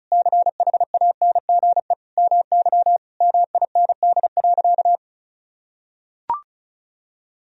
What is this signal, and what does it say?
Morse code, heard as a keyed tone, about 36 wpm, 700 Hz. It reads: CHANGE MY MIND.
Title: Sentences from Top 1000 Words - Review Part 1 - 35wpm